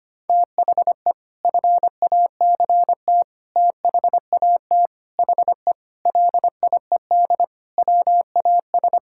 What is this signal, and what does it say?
Morse code, heard as a keyed tone, about 25 wpm, 700 Hz. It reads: THE FACT THAT HE LIED WAS
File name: practice / 25wpm / Sentences with 4-Word Phrases - 25wpm